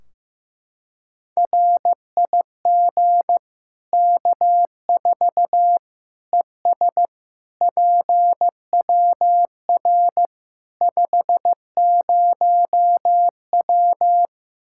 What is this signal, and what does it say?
Morse code, heard as a keyed tone, about 15 wpm, 700 Hz. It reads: RIG K4 ES PWR 50W
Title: Practice QSO - Exchange 2 - Info - 15wpm